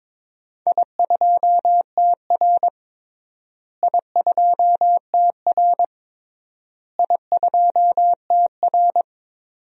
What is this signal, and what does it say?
Morse code, heard as a keyed tone, about 22 wpm, 700 Hz. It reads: I2TR I2TR I2TR
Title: DXCC-100 Call Signs - 4 Characters - Repeated 3x - 22wpm